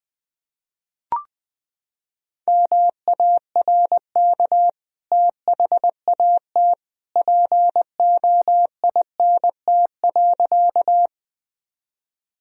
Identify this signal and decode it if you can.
Morse code, heard as a keyed tone, about 20 wpm, 700 Hz. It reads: MARK THAT POINT.
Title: Sentences from Top 1000 Words - Review Part 1 - 20wpm